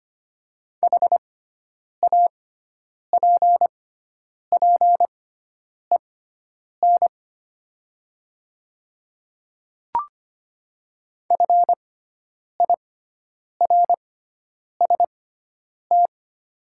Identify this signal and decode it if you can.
Morse code, heard as a keyed tone, about 25 wpm, spaced as for 9 wpm, 690 Hz. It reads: HAPPEN FIRST
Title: 6x Character Spacing - Top 500 Words - 25wpm